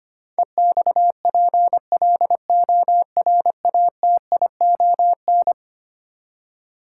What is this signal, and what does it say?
Morse code, heard as a keyed tone, about 25 wpm, 700 Hz. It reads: EXPLORATION
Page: morse-code-ninja-practice-25wpm